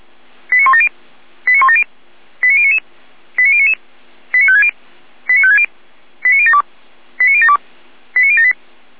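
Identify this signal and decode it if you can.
Five-tone selective call (ZVEI), heard as quick runs of five tones, tones of about 70 ms. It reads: 89170 89170 89099 89099 89560 89560 89072 89072 89078
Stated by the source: ZVEI five-tone call 89169, received by SDR.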